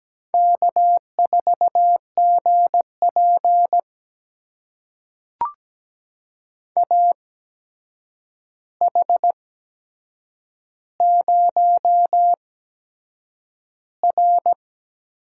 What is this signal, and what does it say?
Morse code, heard as a keyed tone, about 17 wpm, spaced as for 7 wpm, 700 Hz. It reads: K4GP AH0R
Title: USA Call Signs - 8x Character Spacing - 17wpm